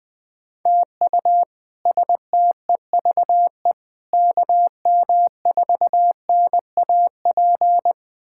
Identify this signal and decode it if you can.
Morse code, heard as a keyed tone, about 20 wpm, 700 Hz. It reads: TU STEVE KM4NAP
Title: K1USN SST QSO Practice - 20wpm